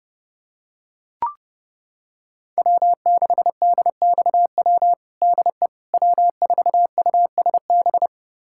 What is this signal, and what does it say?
Morse code, heard as a keyed tone, about 30 wpm, 700 Hz. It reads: W6DXW DE W4USB
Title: USA Callsign DE Callsign - 5 Characters - 30wpm